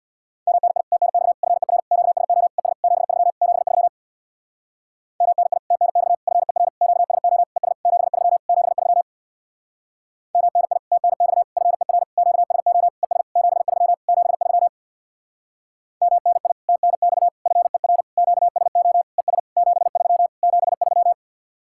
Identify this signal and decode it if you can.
Morse code, heard as a keyed tone, about 63 wpm, 700 Hz. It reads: MNI TNX FER QSO ES 73 73 MNI TNX FER QSO ES 73 73 MNI TNX FER QSO ES 73 73 MNI TNX FER QSO ES 73 73